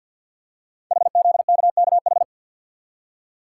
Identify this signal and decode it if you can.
Morse code, heard as a keyed tone, about 50 wpm, 690 Hz. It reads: HZKKH
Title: Code Groups - 5 Letters - 50wpm